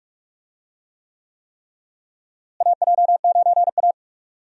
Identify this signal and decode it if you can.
Morse code, heard as a keyed tone, about 46 wpm, 700 Hz. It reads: AJ9A
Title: USA Call Signs - 4 Characters - Repeated 3x - 45wpm